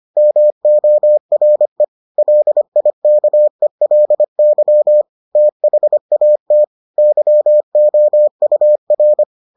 Morse code, heard as a keyed tone, about 25 wpm, 600 Hz. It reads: MORE LIKELY THAT YOUR